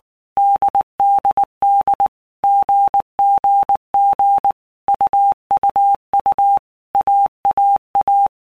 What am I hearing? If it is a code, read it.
Morse code, heard as a keyed tone, about 19 wpm, 800 Hz. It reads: DDD GGG UUU AAA
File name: Basic Letter Groups By Blocks / Groups@18 / AWUNDG@18